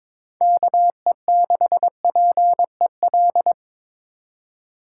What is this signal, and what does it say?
Morse code, heard as a keyed tone, about 26 wpm, 700 Hz. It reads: KE6PEL
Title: Speed Racing - USA 2x3 Call Signs 22wpm